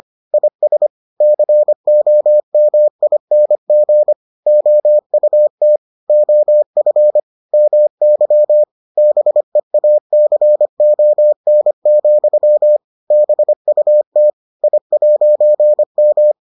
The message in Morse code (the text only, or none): IS COMING OUT OF MY BEACON, BUT I'M